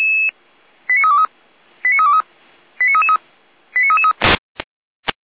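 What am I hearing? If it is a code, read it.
Five-tone selective call (ZVEI), heard as quick runs of five tones, tones of about 70 ms. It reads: 89323 89323 89333 89333
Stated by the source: ZVEI five-tone call 89504 received with an SDR.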